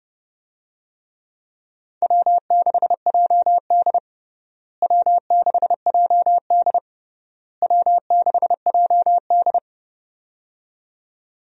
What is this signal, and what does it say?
Morse code, heard as a keyed tone, about 30 wpm, 700 Hz. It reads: W6JD W6JD W6JD
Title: USA Call Signs - 4 Characters - Repeated 3x - 30wpm